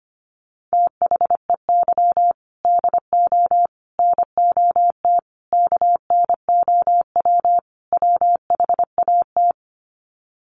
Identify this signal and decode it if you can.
Morse code, heard as a keyed tone, about 25 wpm, 700 Hz. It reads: THEY DO NOT KNOW WHAT
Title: Sets of 5 Words Encore 25wpm